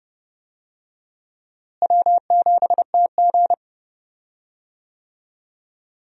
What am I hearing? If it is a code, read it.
Morse code, heard as a keyed tone, about 30 wpm, 700 Hz. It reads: W7TG
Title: USA Call Signs - 4 Characters - Repeated 2x - 30wpm